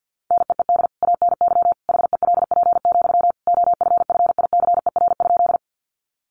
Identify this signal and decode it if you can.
Morse code, heard as a keyed tone, about 50 wpm, 700 Hz. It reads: NEED ANY HELP, OFFICER?
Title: Sentences from Top 1200 Words 50wpm